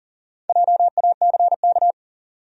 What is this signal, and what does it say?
Morse code, heard as a keyed tone, about 40 wpm, 700 Hz. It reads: JACK
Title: USA Common Names - Max 5 Letters - 40wpm